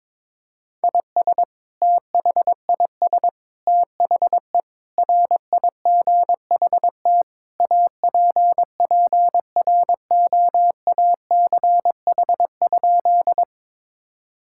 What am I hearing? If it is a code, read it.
Morse code, heard as a keyed tone, about 22 wpm, 700 Hz. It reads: IS THIS THE RIGHT APPROACH?
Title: Sentences from Top 1000 Words - Review Part 6 - 22wpm